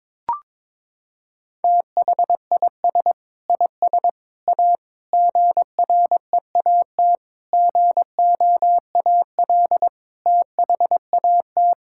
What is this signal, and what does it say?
Morse code, heard as a keyed tone, about 22 wpm, 700 Hz. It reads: THIS IS A GREAT GOAL THAT
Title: Sentences from Top 1000 Words - Review Part 6 - 22wpm